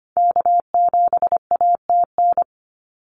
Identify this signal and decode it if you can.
Morse code, heard as a keyed tone, about 25 wpm, 700 Hz. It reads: K7ATN
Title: NA SOTA Activator Callsigns 25wpm